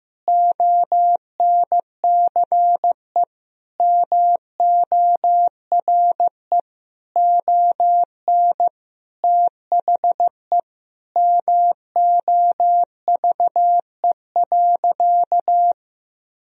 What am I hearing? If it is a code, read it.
Morse code, heard as a keyed tone, about 15 wpm, 700 Hz. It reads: ONCE MORE ON THE MOVE.
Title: Sentences from Top 1000 Words - Review Part 3 - 15wpm